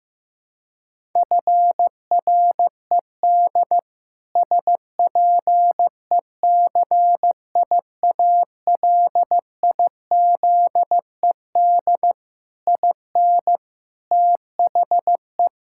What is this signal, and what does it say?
Morse code, heard as a keyed tone, about 15 wpm, 700 Hz. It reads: FRED SPECIALIZED IN THE
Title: Pangram Challenge v2 - 15wpm